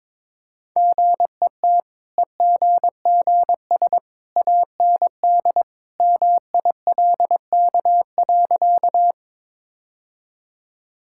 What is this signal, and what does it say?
Morse code, heard as a keyed tone, about 22 wpm, 690 Hz. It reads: GET EGGS AND MILK.
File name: practice / 22wpm / Easy Sentences with Hints - 22wpm